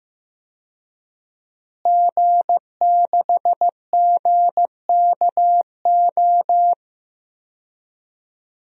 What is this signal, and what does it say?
Morse code, heard as a keyed tone, about 15 wpm, 700 Hz. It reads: G6GKO
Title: DXCC-100 Call Signs - 5 Characters - Repeated 3x - 15wpm